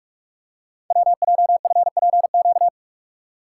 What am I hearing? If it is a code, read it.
Morse code, heard as a keyed tone, about 44 wpm, 700 Hz. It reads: WJFPQ
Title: Code Groups - 5 Letters - 45wpm